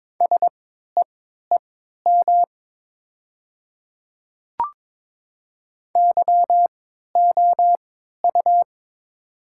Morse code, heard as a keyed tone, about 22 wpm, spaced as for 12 wpm, 700 Hz. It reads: SEEM YOU
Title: 3x Character Spacing - Top 300 Words - 22wpm